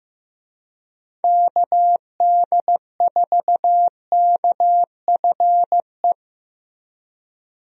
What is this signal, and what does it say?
Morse code, heard as a keyed tone, about 15 wpm, 700 Hz. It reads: KD4KFE
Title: USA Call Signs - 6 Characters - Repeated 3x - 15wpm